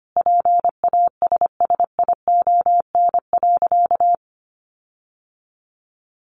Morse code, heard as a keyed tone, about 25 wpm, 700 Hz. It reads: PASSION.
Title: Sentences from Top 1300 Words 25wpm